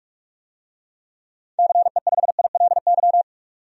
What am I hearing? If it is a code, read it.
Morse code, heard as a keyed tone, about 43 wpm, 700 Hz. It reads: CE5ILY
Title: DXCC-100 Call Signs - 6 Characters - 45wpm